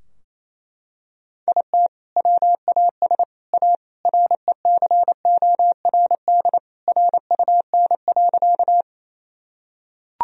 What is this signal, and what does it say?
Morse code, heard as a keyed tone, about 28 wpm, 700 Hz. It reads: IT WAS A RECORD RUN.
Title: Sentences from Top 1000 Words - Review Part 3 - 28wpm